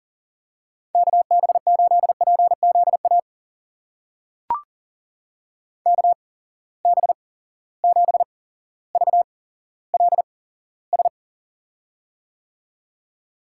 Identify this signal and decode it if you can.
Morse code, heard as a keyed tone, about 40 wpm, spaced as for 17 wpm, 700 Hz. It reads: KB8PZA KB7VLS
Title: USA Call Signs - 6 Characters - 8x Character Spacing - 40wpm